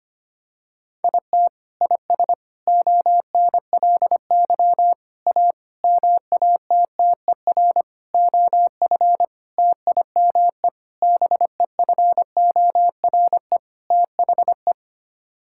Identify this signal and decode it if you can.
Morse code, heard as a keyed tone, about 25 wpm, 700 Hz. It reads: IT IS ONLY A MATTER OF TIME BEFORE THE
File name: Sets of 9 Words - 25wpm